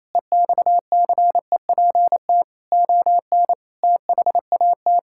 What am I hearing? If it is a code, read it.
Morse code, heard as a keyed tone, about 28 wpm, 700 Hz. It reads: EXCEPT ON THAT
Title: Sentences from Top 1600 Words 28wpm